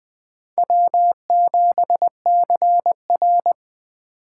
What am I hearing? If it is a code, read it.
Morse code, heard as a keyed tone, about 21 wpm, 700 Hz. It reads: W7CR
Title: Speed Racing - USA 1x2 Call Signs 17wpm